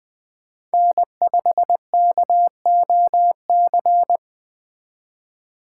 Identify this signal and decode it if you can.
Morse code, heard as a keyed tone, about 24 wpm, 700 Hz. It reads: N5KOC
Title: Speed Racing - USA 1x3 Call Signs 20wpm